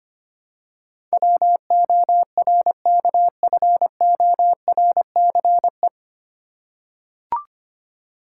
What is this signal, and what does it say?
Morse code, heard as a keyed tone, about 25 wpm, 700 Hz. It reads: WORKFORCE
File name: practice / 25wpm / 9 Letter Words - First 3 Letters Given - 25wpm